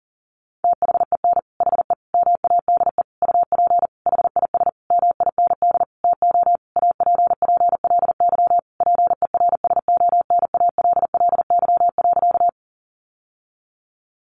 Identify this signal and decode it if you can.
Morse code, heard as a keyed tone, about 40 wpm, 700 Hz. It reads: THEN HE MADE UP HIS MIND TO APPLY PERSONALLY.
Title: Sentences from Top 1400 Words 40wpm